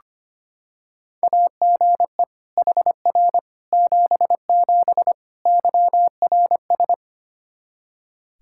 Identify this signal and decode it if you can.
Morse code, heard as a keyed tone, about 25 wpm, 700 Hz. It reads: AGE HR 77 YRS
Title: Practice QSO - Exchange 3 - More Info - 25wpm